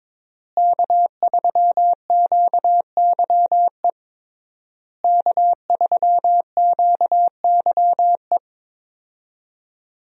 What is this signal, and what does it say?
Morse code, heard as a keyed tone, about 21 wpm, 700 Hz. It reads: K3QYE K3QYE
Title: USA Call Signs - 5 Characters - Repeated 3x - 22wpm